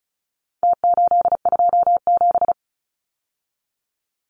Morse code, heard as a keyed tone, about 34 wpm, 700 Hz. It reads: T827
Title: Numbers - Contest Style - 4-Digit 35wpm